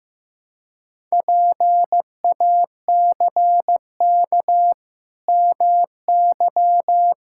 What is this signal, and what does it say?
Morse code, heard as a keyed tone, about 15 wpm, 700 Hz. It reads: PACK MY
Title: Pangram Challenge v2 - 15wpm